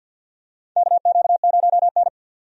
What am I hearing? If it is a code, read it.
Morse code, heard as a keyed tone, about 47 wpm, 700 Hz. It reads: KQ0N
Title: USA Call Signs - 4 Characters - 50wpm